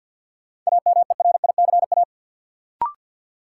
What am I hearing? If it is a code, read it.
Morse code, heard as a keyed tone, about 52 wpm, 700 Hz. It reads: AMERICA
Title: Sets of 5 Words - New Word Drill - 50wpm